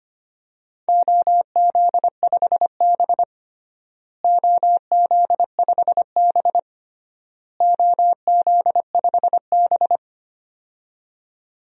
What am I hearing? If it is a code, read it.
Morse code, heard as a keyed tone, about 25 wpm, 700 Hz. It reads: OZ5B OZ5B OZ5B
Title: DXCC-100 Call Signs - 4 Characters - Repeated 3x - 25wpm